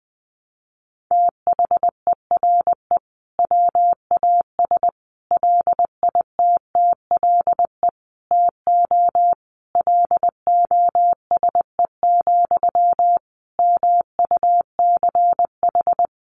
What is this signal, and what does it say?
Morse code, heard as a keyed tone, about 20 wpm, 700 Hz. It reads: THERE WAS LITTLE TO LOSE, MUCH
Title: Sentences from Top 1100 Words 20wpm